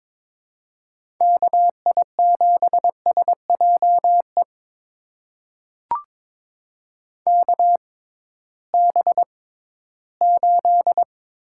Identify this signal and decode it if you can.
Morse code, heard as a keyed tone, about 22 wpm, spaced as for 14 wpm, 700 Hz. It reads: KI7SJE KB8
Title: USA Call Signs - 6x Character Spacing - 22wpm